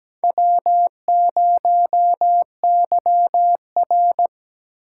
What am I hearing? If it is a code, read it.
Morse code, heard as a keyed tone, about 17 wpm, 700 Hz. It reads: W0YR
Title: K1USN SST QSO Practice - 17wpm